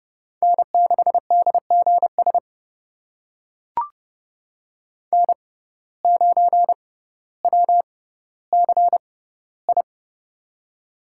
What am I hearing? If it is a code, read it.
Morse code, heard as a keyed tone, about 30 wpm, spaced as for 15 wpm, 700 Hz. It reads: N6DGS N9WCI
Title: USA Call Signs - 5 Characters - 6x Character Spacing - 30wpm